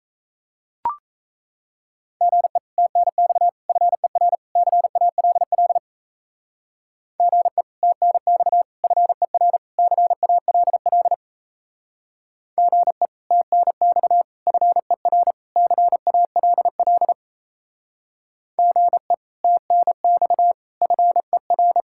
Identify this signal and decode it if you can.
Morse code, heard as a keyed tone, about 34 wpm, 700 Hz. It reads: GE TNX FER CALL GE TNX FER CALL GE TNX FER CALL GE TNX FER
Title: Speed-Racing Common QSO Phrases - 28wpm